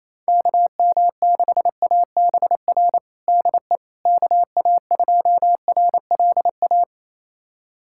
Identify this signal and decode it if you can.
Morse code, heard as a keyed tone, about 28 wpm, 700 Hz. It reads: KM6ABR DE KA2RLA
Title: USA Callsign DE Callsign 28wpm